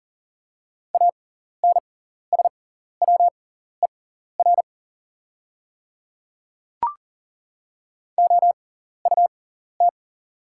Morse code, heard as a keyed tone, about 40 wpm, spaced as for 14 wpm, 700 Hz. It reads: ANSWER OUT
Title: 6x Character Spacing - Top 300 Words - 40wpm